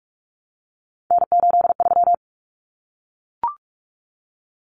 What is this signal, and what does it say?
Morse code, heard as a keyed tone, about 44 wpm, 700 Hz. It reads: N83
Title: Common Cut Numbers - 3-Digit 45wpm